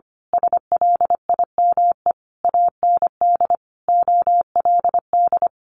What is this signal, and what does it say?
Morse code, heard as a keyed tone, about 25 wpm, 700 Hz. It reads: SLIME AND OLD
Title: A Sound Of Thunder 25wpm